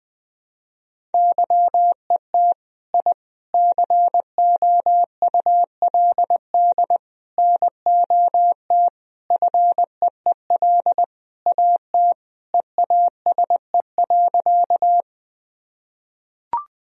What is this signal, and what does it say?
Morse code, heard as a keyed tone, about 20 wpm, 700 Hz. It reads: YET I COULD NOT FEEL AT EASE.
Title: Sentences from Top 1000 Words - Review Part 1 - 20wpm